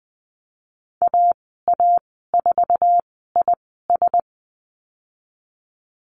Morse code, heard as a keyed tone, about 20 wpm, spaced as for 15 wpm, 700 Hz. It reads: AA4IS
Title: 088 Lesson 15g - Number 5 - Call Signs 20wpm